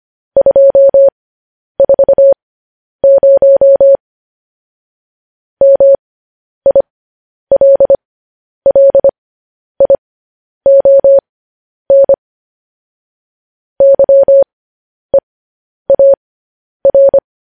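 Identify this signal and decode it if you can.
Morse code, heard as a keyed tone, about 25 wpm, spaced as for 10 wpm, 550 Hz. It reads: 240 MILLION YEAR
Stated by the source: Morse code news headlines transmission.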